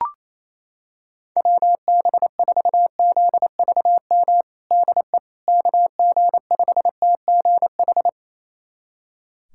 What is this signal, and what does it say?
Morse code, heard as a keyed tone, about 28 wpm, 700 Hz. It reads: WB4ZVM DE KG5TGH
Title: USA Callsign DE Callsign - 6 Characters - 28wpm